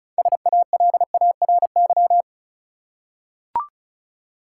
Morse code, heard as a keyed tone, about 36 wpm, 700 Hz. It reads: SALARY
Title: Six Letter Words - 35wpm